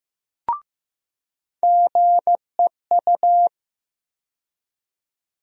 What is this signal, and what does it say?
Morse code, heard as a keyed tone, about 15 wpm, 700 Hz. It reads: GEU
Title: USA Call Sign Postfixes - 3 Letters 15wpm